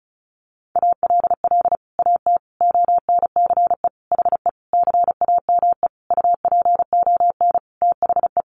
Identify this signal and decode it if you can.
Morse code, heard as a keyed tone, about 34 wpm, 700 Hz. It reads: ALL AT ONCE HE CAME UPON THE
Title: Sentences from Top 1100 Words 35wpm